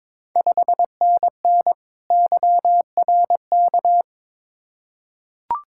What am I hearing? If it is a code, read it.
Morse code, heard as a keyed tone, about 22 wpm, 700 Hz. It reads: 5NN YRK
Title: Ontario QSO Party Exchange 22wpm